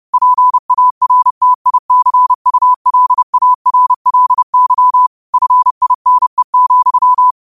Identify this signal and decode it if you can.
Morse code, heard as a keyed tone, about 30 wpm, 1000 Hz. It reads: PARTICULARLY FINE,